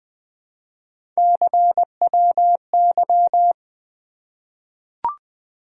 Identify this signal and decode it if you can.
Morse code, heard as a keyed tone, about 20 wpm, 700 Hz. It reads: CWY
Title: Three Letter-Number Combinations - 20wpm